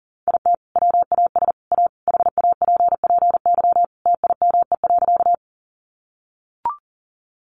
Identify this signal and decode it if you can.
Morse code, heard as a keyed tone, about 40 wpm, 700 Hz. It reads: IT WAS A HAPPY TIME.
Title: Sentences from Top 1000 Words 40wpm